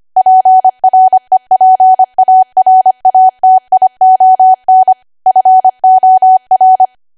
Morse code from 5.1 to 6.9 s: FOR

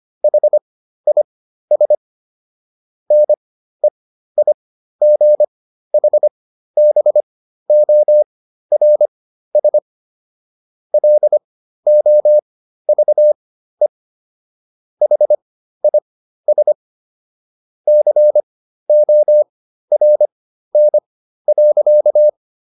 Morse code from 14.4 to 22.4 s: HIS CORN.

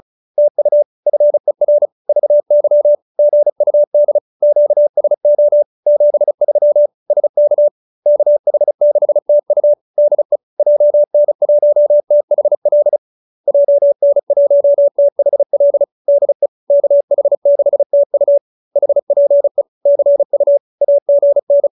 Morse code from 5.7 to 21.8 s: 73 SK KH6TU DE JN1THL JN1THL DE KH6TU HPE CU AGN

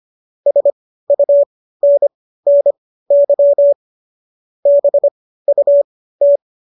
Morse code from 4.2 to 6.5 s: BUT

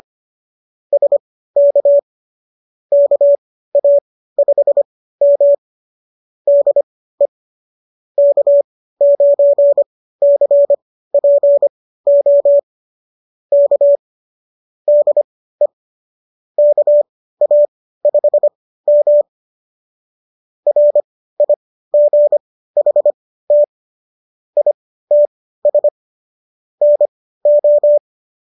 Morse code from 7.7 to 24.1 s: K9CPO K DE KA5M RIGHT

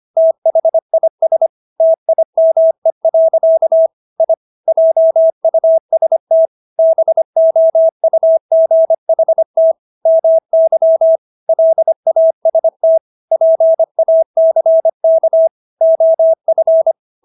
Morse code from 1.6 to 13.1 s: TIME. I JUST BOUGHT MY LAST

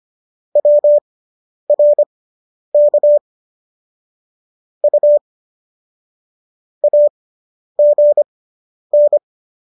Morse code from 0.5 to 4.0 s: WRK

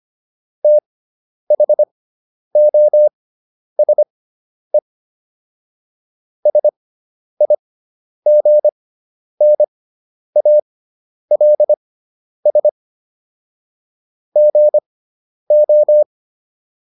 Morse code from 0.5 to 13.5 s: THOSE SIGNALS

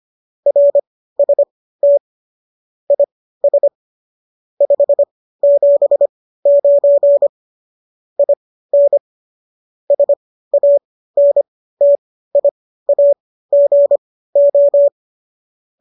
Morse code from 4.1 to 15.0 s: 579 IN SANTIAGO

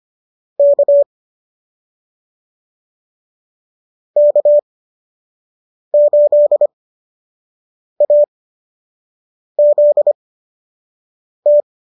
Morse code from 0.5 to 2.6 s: K

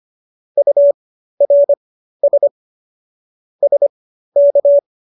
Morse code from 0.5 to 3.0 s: URS

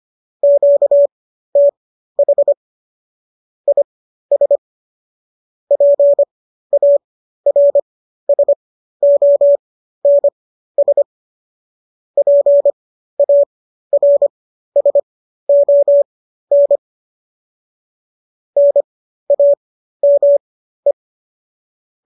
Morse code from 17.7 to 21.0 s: NAME

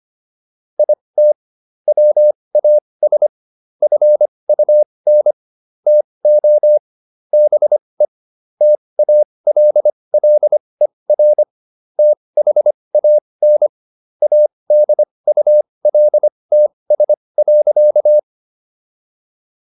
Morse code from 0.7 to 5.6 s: IT WAS FUN

